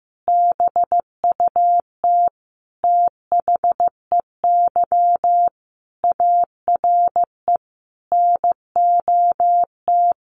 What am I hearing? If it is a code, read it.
Morse code, heard as a keyed tone, about 14 wpm, 700 Hz. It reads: BUT THEY ARE NOT
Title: Sentences from Top 1400 Words 15wpm